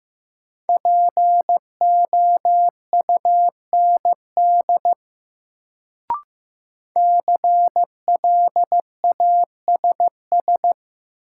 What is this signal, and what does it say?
Morse code, heard as a keyed tone, about 15 wpm, 700 Hz. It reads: POUND CLASS
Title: Sentences from Top 1000 Words - Review Part 2 - New Word Drill - 15wpm